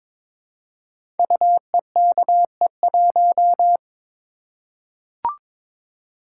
Morse code, heard as a keyed tone, about 22 wpm, 700 Hz. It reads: UEKE1
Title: Code Groups - 5 Letter:Numbers - 22wpm